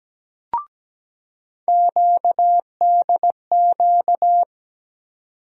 Morse code, heard as a keyed tone, about 17 wpm, 700 Hz. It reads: QDQ
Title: USA Call Sign Postfixes - 3 Letters 17wpm